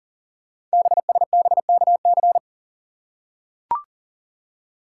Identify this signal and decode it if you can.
Morse code, heard as a keyed tone, about 41 wpm, 690 Hz. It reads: BSBKC
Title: Code Groups - 5 Letters - 40wpm